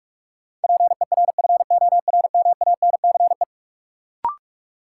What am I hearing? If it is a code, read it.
Morse code, heard as a keyed tone, about 47 wpm, 700 Hz. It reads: PERFORMANCE
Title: Sentences from Top 1000 Words - New Word Drill - 45wpm